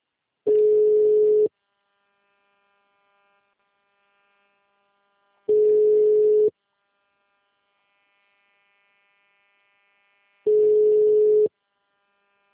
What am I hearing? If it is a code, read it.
Ringback tone (Europe, 425 Hz).